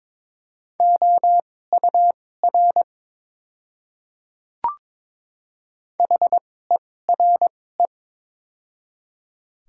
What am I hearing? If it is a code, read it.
Morse code, heard as a keyed tone, about 22 wpm, spaced as for 16 wpm, 700 Hz. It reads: OUR HERE
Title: 2x Character Spacing - Top 200 Words - 22wpm